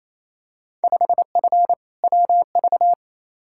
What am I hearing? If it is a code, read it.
Morse code, heard as a keyed tone, about 28 wpm, 700 Hz. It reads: 5F WV
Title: Field Day Exchanges Encore - 28wpm